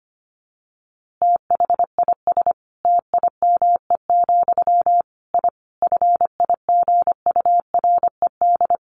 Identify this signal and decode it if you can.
Morse code, heard as a keyed tone, about 25 wpm, 700 Hz. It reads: THIS TIME, I FIGURED
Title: A Sound Of Thunder 25wpm